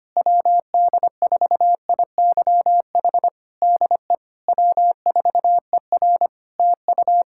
Morse code, heard as a keyed tone, about 25 wpm, 700 Hz. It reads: WD4IYH DE W4ER TU
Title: Practice QSO - Exchange 4 - Ending - 25wpm